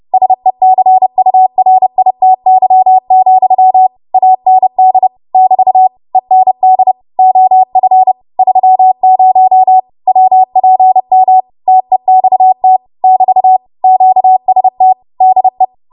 Morse code, heard as a keyed tone, about 30 wpm, 750 Hz. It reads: SECURITY, AND <BT> END OF 30 WPM TEXT <BT> QST DE
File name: Morse / 30WPM